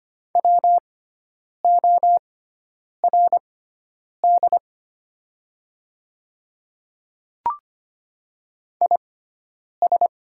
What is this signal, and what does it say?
Morse code, heard as a keyed tone, about 25 wpm, spaced as for 9 wpm, 700 Hz. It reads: WORD IS
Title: 6x Character Spacing - Top 100 Words - 25wpm